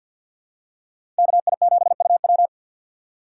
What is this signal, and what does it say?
Morse code, heard as a keyed tone, about 52 wpm, 700 Hz. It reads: KI7UW
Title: USA Call Signs - 5 Characters - 3x Character Spacing - 50wpm